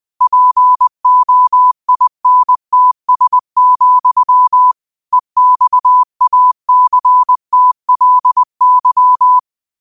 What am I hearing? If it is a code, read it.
Morse code, heard as a keyed tone, about 20 wpm, 1000 Hz. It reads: POINTS, EXACTLY
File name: qotdfiles / 20wpm